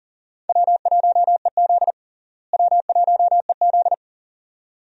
Morse code, heard as a keyed tone, about 40 wpm, 700 Hz. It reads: W1EZ W1EZ
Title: USA Call Signs - 4 Characters - Repeated 2x - 40wpm